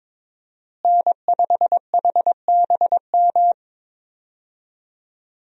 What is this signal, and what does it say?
Morse code, heard as a keyed tone, about 22 wpm, 700 Hz. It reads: N5HBM
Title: USA Call Signs - 5 Characters - Repeated 3x - 22wpm